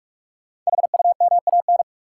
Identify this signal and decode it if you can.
Morse code, heard as a keyed tone, about 44 wpm, 700 Hz. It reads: HUMAN
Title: Sets of 5 Words - New Word Drill - 45wpm